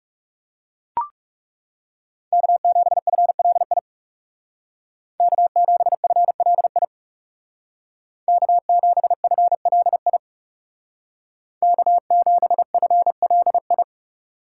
Morse code, heard as a keyed tone, about 36 wpm, 700 Hz. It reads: K7FLI K7FLI K7FLI K7FLI
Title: Speed Racing - USA 1x3 Call Signs 30wpm